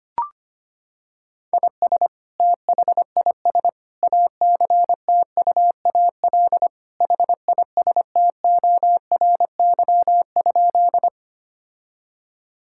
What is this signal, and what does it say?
Morse code, heard as a keyed tone, about 25 wpm, 700 Hz. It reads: IS THIS ACTUAL HISTORY?
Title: Sentences from Top 1000 Words - Review Part 5 - 25wpm